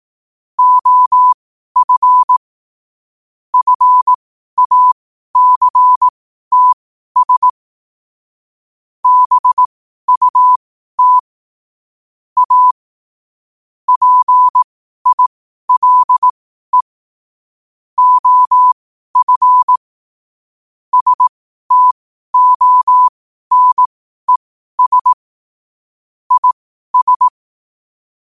Morse code, heard as a keyed tone, about 18 wpm, spaced as for 12 wpm, 1000 Hz. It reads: OF FACTS BUT A PILE OF STONES IS
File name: qotdfiles / 12wpm